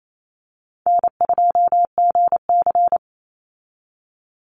Morse code, heard as a keyed tone, about 28 wpm, 700 Hz. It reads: N2GC
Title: Easy Call Signs - 28wpm